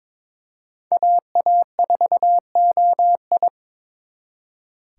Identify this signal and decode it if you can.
Morse code, heard as a keyed tone, about 26 wpm, 700 Hz. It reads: AA4OI
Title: Speed Racing - USA 2x2 Call Signs 22wpm